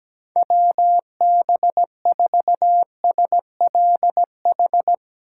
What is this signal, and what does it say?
Morse code, heard as a keyed tone, about 18 wpm, 700 Hz. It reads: WB4SLH
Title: Speed Racing - USA 2x3 Call Signs 17wpm